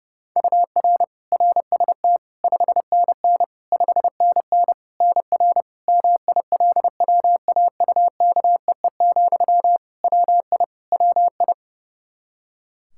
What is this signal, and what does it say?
Morse code, heard as a keyed tone, about 30 wpm, 700 Hz. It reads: UR RST 5NN 5NN NR MILWAUKEE, WI WI
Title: Practice QSO - Exchange 1 - Beginning - 30wpm